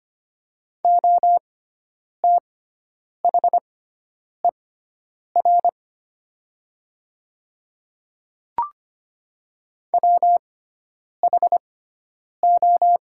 Morse code, heard as a keyed tone, about 25 wpm, spaced as for 9 wpm, 700 Hz. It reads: OTHER WHO
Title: 6x Character Spacing - Top 200 Words - 25wpm